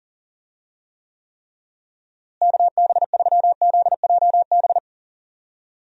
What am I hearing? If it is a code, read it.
Morse code, heard as a keyed tone, about 37 wpm, 700 Hz. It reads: KB3ZJB
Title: USA Call Signs - 6 Characters - Repeated 3x - 40wpm